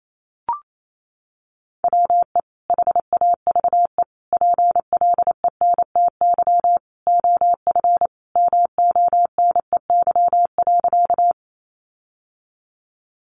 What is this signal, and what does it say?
Morse code, heard as a keyed tone, about 28 wpm, 700 Hz. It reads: WE HAVE PLENTY OF MONEY.
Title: Sentences from Top 1300 Words 28wpm